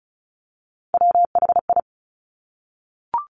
Morse code, heard as a keyed tone, about 36 wpm, 700 Hz. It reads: WHI
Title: Top Three Letter Combinations in Words 35wpm